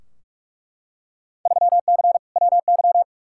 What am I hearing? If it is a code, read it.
Morse code, heard as a keyed tone, about 47 wpm, 700 Hz. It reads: 3C WY
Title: Field Day Exchanges Encore - 45wpm